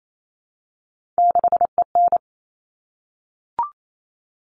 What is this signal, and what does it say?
Morse code, heard as a keyed tone, about 28 wpm, 700 Hz. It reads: 6EN
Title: All Cut Numbers - 3-Digit 28wpm